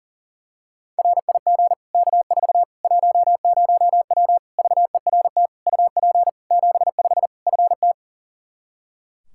Morse code, heard as a keyed tone, about 40 wpm, 700 Hz. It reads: RIG K4 10W VERT UP 75 FT
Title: Practice QSO - Exchange 2 - Info - 40wpm